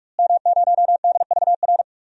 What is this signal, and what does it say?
Morse code, heard as a keyed tone, about 48 wpm, 700 Hz. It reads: M0DVR
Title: DXCC-100 Call Signs - 5 Characters - Repeated 2x - 45wpm